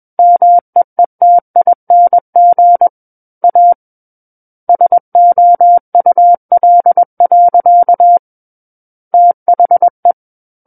Morse code, heard as a keyed tone, about 21 wpm, 700 Hz. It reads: MEETING A SOUL. THE
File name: Book One / 12-Chapter 12 - 21 WPM